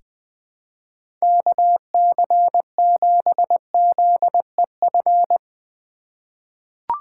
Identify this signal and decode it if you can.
Morse code, heard as a keyed tone, about 21 wpm, 700 Hz. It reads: KC7ZEF
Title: USA Call Signs - 6 Characters - 3x Character Spacing - 20wpm